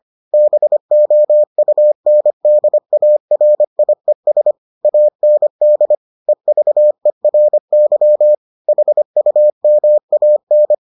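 Morse code, heard as a keyed tone, about 25 wpm, 600 Hz. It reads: BOUNDARIES AND EVERY HUMAN
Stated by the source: Morse edition of the Foundations of Amateur Radio episodes.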